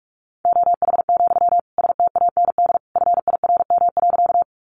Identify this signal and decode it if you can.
Morse code, heard as a keyed tone, about 44 wpm, 700 Hz. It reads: OH, STAND FIRM.
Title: Sentences from Top 1400 Words 45wpm